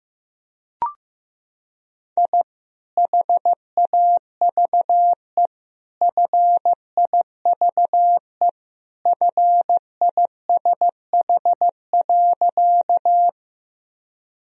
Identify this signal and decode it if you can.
Morse code, heard as a keyed tone, about 15 wpm, 700 Hz. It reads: I HAVE FIVE FISH.
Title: Sentences from Top 1000 Words - Review Part 2 - 15wpm